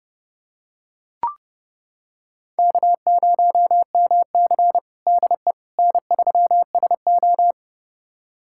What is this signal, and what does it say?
Morse code, heard as a keyed tone, about 31 wpm, 700 Hz. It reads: K0MC DE N3SO
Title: USA Callsign DE Callsign - 4 Characters - 30wpm